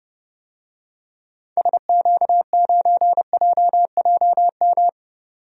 Morse code, heard as a keyed tone, about 32 wpm, 700 Hz. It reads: SQ9JJM
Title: DXCC-100 Call Signs - 6 Characters - 30wpm